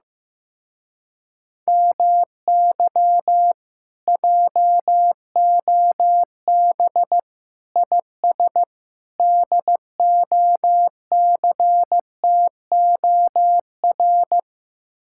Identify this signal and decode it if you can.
Morse code, heard as a keyed tone, about 15 wpm, 700 Hz. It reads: MY JOB IS DOCTOR
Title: Speed-Racing Common QSO Phrases Encore - 15wpm